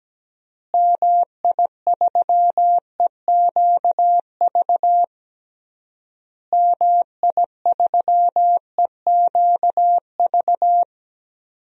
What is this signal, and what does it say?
Morse code, heard as a keyed tone, about 17 wpm, 700 Hz. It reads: MI3EQV MI3EQV
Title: DXCC-100 Call Signs - 6 Characters - Repeated 3x - 17wpm